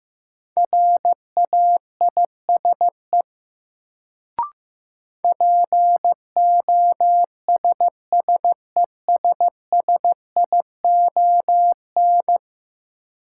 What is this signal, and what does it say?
Morse code, heard as a keyed tone, about 15 wpm, 700 Hz. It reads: RAISE POSSESSION
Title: Sentences from Top 1400 Words - New Word Drill - 15wpm